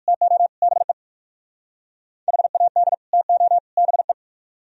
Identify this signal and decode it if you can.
Morse code, heard as a keyed tone, about 56 wpm, 700 Hz. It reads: TO BE HAD TO BE